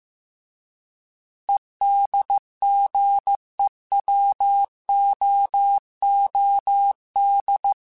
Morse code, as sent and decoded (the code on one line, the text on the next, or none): . -.. --. .-- --- --- -..
EDGEWOOD